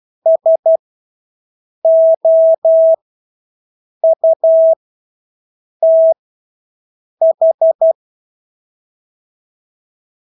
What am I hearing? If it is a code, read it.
Morse code, heard as a keyed tone, about 12 wpm, spaced as for 6 wpm, 650 Hz. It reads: SOUTH
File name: Engelske ord 12-6 wpm